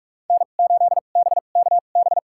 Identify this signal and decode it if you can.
Morse code, heard as a keyed tone, about 42 wpm, spaced as for 31 wpm, 700 Hz. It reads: N8BKB
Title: USA Call Signs - 5 Characters - 2x Character Spacing - 45wpm